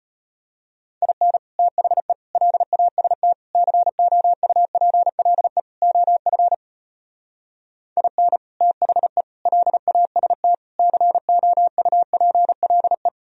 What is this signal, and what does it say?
Morse code, heard as a keyed tone, about 33 wpm, 700 Hz. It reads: IN THE LAST COUPLE OF IN THE LAST COUPLE